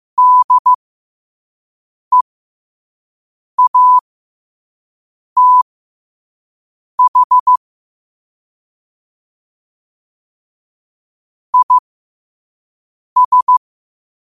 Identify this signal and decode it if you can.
Morse code, heard as a keyed tone, about 15 wpm, spaced as for 5 wpm, 1000 Hz. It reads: DEATH IS
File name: qotdfiles / 5wpm